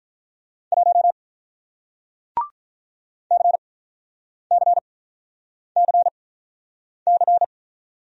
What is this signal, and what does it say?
Morse code, heard as a keyed tone, about 48 wpm, spaced as for 9 wpm, 700 Hz. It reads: 1 CCCC